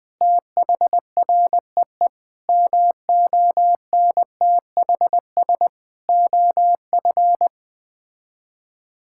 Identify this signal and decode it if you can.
Morse code, heard as a keyed tone, about 20 wpm, 700 Hz. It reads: THREE MONTHS OF
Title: Sets of 6 Words - 20wpm